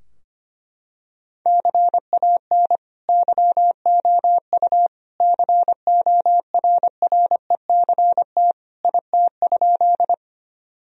Morse code, heard as a keyed tone, about 25 wpm, 700 Hz. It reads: CAN YOU CORRECT IT?
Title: Sentences from Top 1000 Words - Review Part 2 - 25wpm